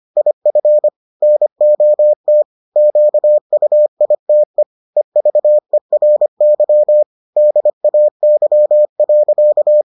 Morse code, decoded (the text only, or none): IF NOT QUITE EVERY DAY.